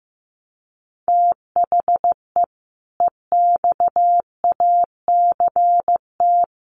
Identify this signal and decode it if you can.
Morse code, heard as a keyed tone, about 15 wpm, 700 Hz. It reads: THE EXACT
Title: Sentences from Top 1300 Words 15wpm